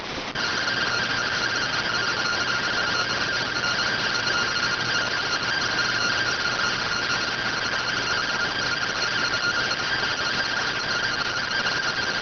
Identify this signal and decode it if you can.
Two-tone teleprinter signal, about 170 Hz shift.